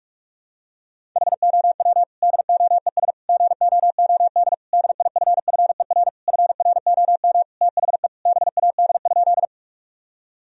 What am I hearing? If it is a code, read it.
Morse code, heard as a keyed tone, about 46 wpm, 700 Hz. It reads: HOW DOES GOOD DIFFER FROM THE BAD?
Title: Sentences from Top 1000 Words - Review Part 1 - 45wpm